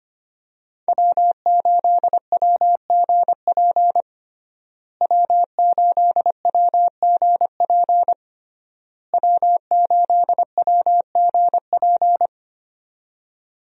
Morse code, heard as a keyed tone, about 25 wpm, 700 Hz. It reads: W8WGP W8WGP W8WGP
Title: USA Call Signs - 5 Characters - Repeated 3x - 25wpm